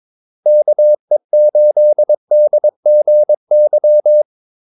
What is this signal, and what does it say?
Morse code, heard as a keyed tone, about 22 wpm, 600 Hz. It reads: KE8DGY
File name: Callsigns-22wpm